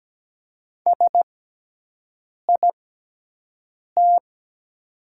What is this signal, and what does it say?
Morse code, heard as a keyed tone, about 17 wpm, spaced as for 6 wpm, 700 Hz. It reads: SIT
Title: 6x Character Spacing - Top 500 Words - 17wpm